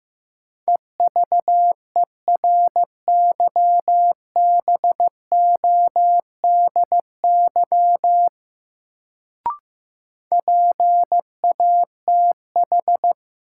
Morse code, heard as a keyed tone, about 15 wpm, 700 Hz. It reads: EVERYBODY PATH
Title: Sentences from Top 1400 Words - New Word Drill Alt - 15wpm